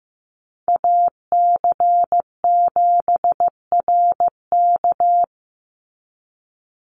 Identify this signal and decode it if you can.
Morse code, heard as a keyed tone, about 15 wpm, 700 Hz. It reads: AC7RK
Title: USA Call Signs 15wpm